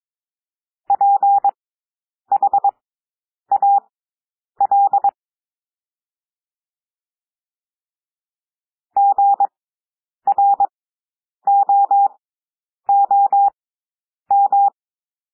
Morse code, heard as a keyed tone, about 22 wpm, spaced as for 9 wpm, 800 Hz. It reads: PHAL GROOM
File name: cwmp3 / W2 SOUND WORDS 22 wpm